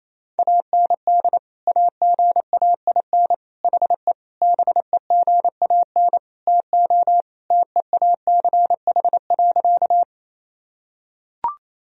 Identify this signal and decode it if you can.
Morse code, heard as a keyed tone, about 28 wpm, 700 Hz. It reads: AND AGAIN HE BEGAN TO TEACH.
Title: Sentences from Top 1000 Words - Review Part 2 - 28wpm